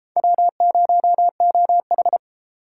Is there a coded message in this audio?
Yes: Morse code from 0.2 to 2.2 s, heard as a keyed tone, about 33 wpm, 700 Hz. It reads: W0OH